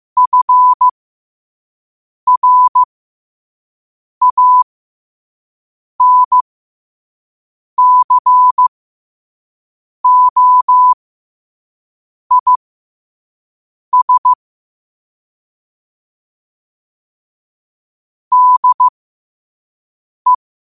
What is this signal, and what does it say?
Morse code, heard as a keyed tone, about 15 wpm, spaced as for 5 wpm, 1000 Hz. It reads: FRANCOIS DE
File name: qotdfiles / 5wpm